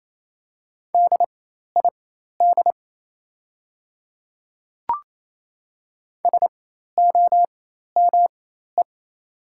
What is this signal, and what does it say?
Morse code, heard as a keyed tone, about 28 wpm, spaced as for 13 wpm, 700 Hz. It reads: DID SOME
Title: 4x Character Spacing - Top 100 Words - 28wpm